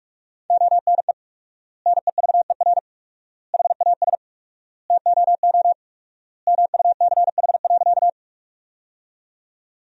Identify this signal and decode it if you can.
Morse code, heard as a keyed tone, about 46 wpm, 700 Hz. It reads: ONE NEVER HAS TOO MUCH.
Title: 4x Word-Spacing - Sentences from Top 300 Words - 45wpm